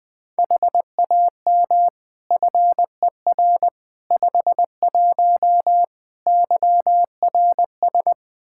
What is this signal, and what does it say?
Morse code, heard as a keyed tone, about 20 wpm, 700 Hz. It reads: HAM FER 51 YRS
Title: Practice QSO - Exchange 3 - More Info - 20wpm